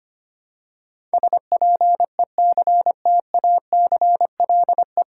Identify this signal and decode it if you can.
Morse code, heard as a keyed tone, about 25 wpm, 690 Hz. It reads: SPECTACLE
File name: practice / 25wpm / 9 Letter Words - First 3 Letters Given - 25wpm